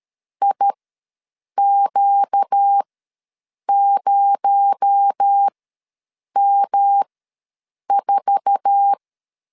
Morse code, heard as a keyed tone, about 13 wpm, spaced as for 7 wpm, 800 Hz. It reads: IQ0M4
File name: LEZIONI / 8 YJ45 / 8 YJ4 ws 12wpm 3249